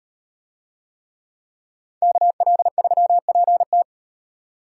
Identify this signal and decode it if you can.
Morse code, heard as a keyed tone, about 36 wpm, 700 Hz. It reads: KL3PT